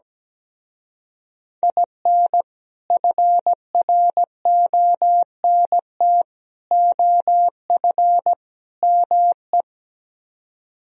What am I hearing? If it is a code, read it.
Morse code, heard as a keyed tone, about 17 wpm, 700 Hz. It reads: IN FRONT OF ME
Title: Speed-Racing - Sets of 4 Words 17wpm